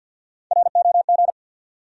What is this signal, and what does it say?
Morse code, heard as a keyed tone, about 52 wpm, 700 Hz. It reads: ROG
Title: USA Common Names - 50wpm